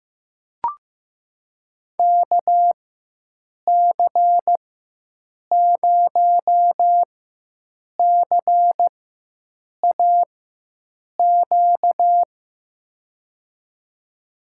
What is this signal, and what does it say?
Morse code, heard as a keyed tone, about 15 wpm, spaced as for 7 wpm, 700 Hz. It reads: KC0CAQ
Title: USA Call Signs - 6 Characters - 4x Character Spacing - 15wpm